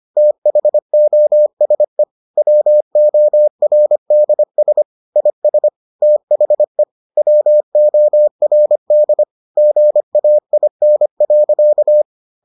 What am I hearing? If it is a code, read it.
Morse code, heard as a keyed tone, about 25 wpm, 600 Hz. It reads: THOSE WORDS IS THE WORD GAIN.